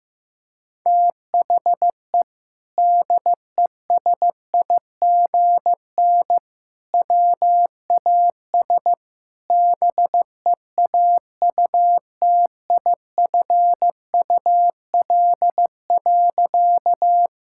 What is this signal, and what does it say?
Morse code, heard as a keyed tone, about 15 wpm, 700 Hz. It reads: THE DESIGN WAS BEAUTIFUL.
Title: Sentences from Top 1000 Words - Review Part 5 - 15wpm